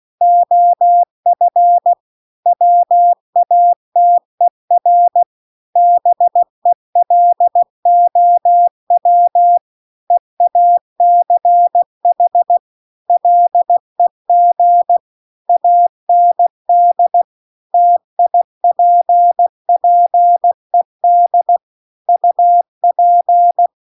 Morse code, heard as a keyed tone, about 16 wpm, 700 Hz. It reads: OF WATER BELOW EACH LEG AND TIPPED UP